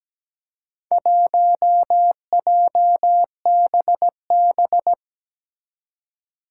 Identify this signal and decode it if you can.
Morse code, heard as a keyed tone, about 17 wpm, 700 Hz. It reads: 1JBB
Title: Four Challenging Letter-Number Combinations 17wpm